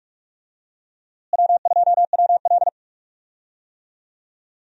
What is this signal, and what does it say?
Morse code, heard as a keyed tone, about 44 wpm, 700 Hz. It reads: W2WL
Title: USA Call Signs - 4 Characters - Repeated 3x - 45wpm